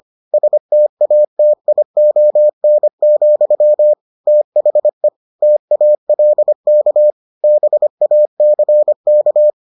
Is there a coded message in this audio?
Yes: Morse code from 0.3 to 9.5 s, heard as a keyed tone, about 25 wpm, 600 Hz. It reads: STATION, THE TALK BACK